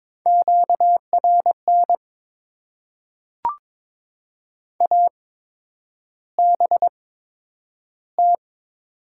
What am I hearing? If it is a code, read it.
Morse code, heard as a keyed tone, about 22 wpm, spaced as for 9 wpm, 690 Hz. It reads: QRN ABT
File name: Top 90 QSO Elements - 8x Character Spacing - 22wpm